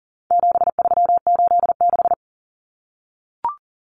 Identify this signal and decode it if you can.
Morse code, heard as a keyed tone, about 40 wpm, 700 Hz. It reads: 7386
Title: Common Cut Numbers - 4-Digit 40wpm